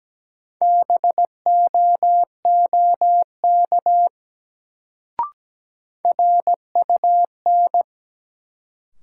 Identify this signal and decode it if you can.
Morse code, heard as a keyed tone, about 17 wpm, 700 Hz. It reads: BOOK RUN
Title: Sentences from Top 300 Words - New Word Drill - 17wpm